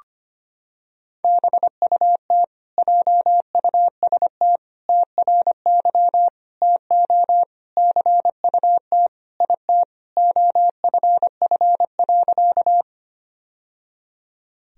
Morse code, heard as a keyed tone, about 25 wpm, 700 Hz. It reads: BUT JUST TRY TO CUT IT OFF.
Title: Sentences from Top 1000 Words - Review Part 1 - 25wpm